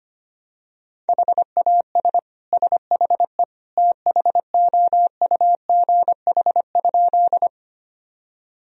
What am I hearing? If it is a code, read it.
Morse code, heard as a keyed tone, about 25 wpm, 700 Hz. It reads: HAS SHE THOUGH?
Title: Sentences from Top 1000 Words - Review Part 2 - 25wpm